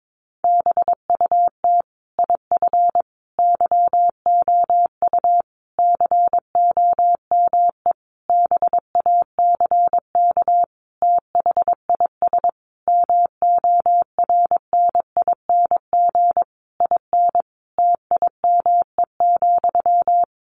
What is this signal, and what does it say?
Morse code, heard as a keyed tone, about 22 wpm, 700 Hz. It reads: BUT IF YOU COME BACK THIS MORNING IN TIME,
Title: A Sound Of Thunder 22wpm